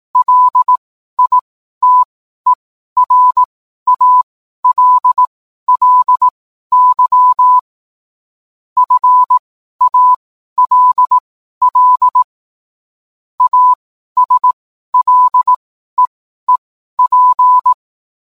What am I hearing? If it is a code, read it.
Morse code, heard as a keyed tone, about 18 wpm, spaced as for 12 wpm, 1000 Hz. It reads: LITERALLY FALL ASLEEP